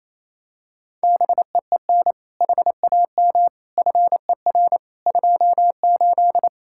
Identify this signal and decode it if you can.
Morse code, heard as a keyed tone, about 27 wpm, 700 Hz. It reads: BEEN HAM FER 28
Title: K1USN SST QSO Practice 28wpm